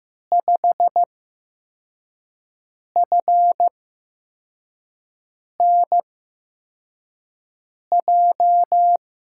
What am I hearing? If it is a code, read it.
Morse code, heard as a keyed tone, about 15 wpm, spaced as for 4 wpm, 700 Hz. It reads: 5FNJ
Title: USA Call Signs - 6 Characters - 8x Character Spacing - 15wpm